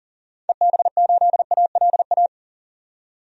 Morse code, heard as a keyed tone, about 40 wpm, 700 Hz. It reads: EB8ALA
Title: DXCC-100 Call Signs - 6 Characters - 40wpm